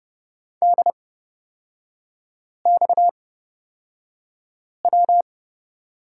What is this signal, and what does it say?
Morse code, heard as a keyed tone, about 30 wpm, spaced as for 5 wpm, 690 Hz. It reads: DXW
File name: Lesson 37e - Letter x - Three Characters 30wpm